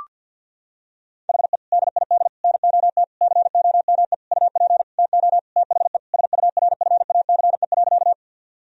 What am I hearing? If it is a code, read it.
Morse code, heard as a keyed tone, about 50 wpm, 700 Hz. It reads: HE DID NOT COME UP TO THE SURFACE.
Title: Sentences from Top 1000 Words - Review Part 3 - 50wpm